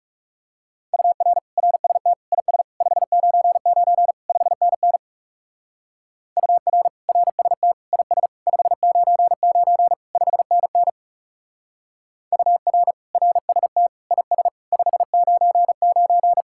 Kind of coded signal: Morse code

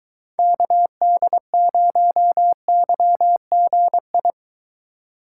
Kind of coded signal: Morse code

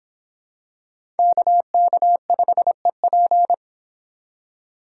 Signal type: Morse code

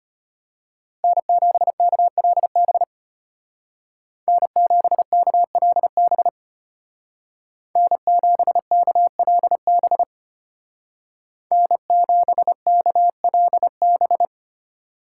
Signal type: Morse code

